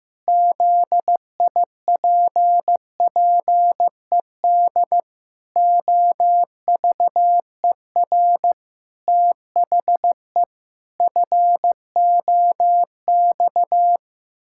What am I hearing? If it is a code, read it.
Morse code, heard as a keyed tone, about 15 wpm, 700 Hz. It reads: ZIPPED OVER THE FOX
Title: Pangram Challenge v2 - 15wpm